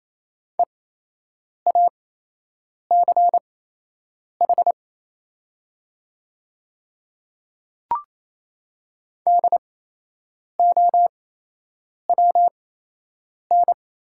Morse code, heard as a keyed tone, about 27 wpm, spaced as for 8 wpm, 700 Hz. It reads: EACH DOWN
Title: 8x Character Spacing - Top 100 Words - 28wpm